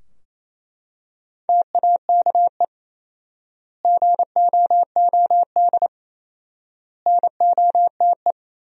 Morse code, heard as a keyed tone, about 28 wpm, 700 Hz. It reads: TAKE GOOD NOTE
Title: 4x Word-Spacing - Sentences from Top 500 Words - 28wpm